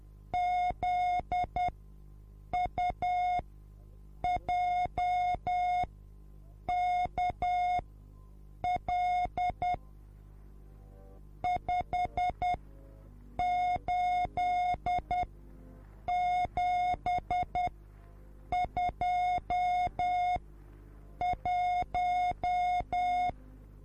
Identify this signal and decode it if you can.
Morse code, heard as a keyed tone, about 10 wpm, spaced as for 7 wpm, 720 Hz. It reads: ZUJKL 58721
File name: repos / Morsekurs / DJ4UF Morsen24